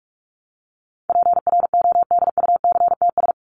Morse code, heard as a keyed tone, about 44 wpm, 700 Hz. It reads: PRODUCTS
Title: ES:S Suffix - Top 100 Words with Base Words 45wpm